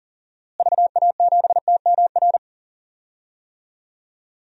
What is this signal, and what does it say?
Morse code, heard as a keyed tone, about 38 wpm, 700 Hz. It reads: VA7TMR
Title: DXCC-100 Call Signs - 6 Characters - Repeated 3x - 40wpm